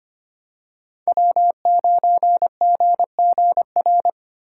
Morse code, heard as a keyed tone, about 25 wpm, 700 Hz. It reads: W9GGR
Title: USA Call Signs - 5 Characters - 6x Character Spacing - 25wpm